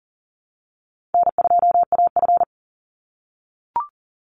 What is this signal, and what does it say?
Morse code, heard as a keyed tone, about 40 wpm, 700 Hz. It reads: N2AF
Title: NA SOTA Chaser Callsigns 40wpm